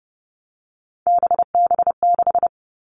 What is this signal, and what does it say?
Morse code, heard as a keyed tone, about 30 wpm, 700 Hz. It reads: BB6
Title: All Cut Numbers - 3-Digit 30wpm